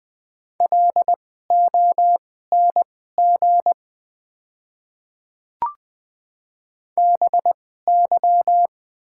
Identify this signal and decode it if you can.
Morse code, heard as a keyed tone, about 20 wpm, spaced as for 14 wpm, 700 Hz. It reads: LONG BY
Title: 2x Character Spacing - Top 100 Words - 20wpm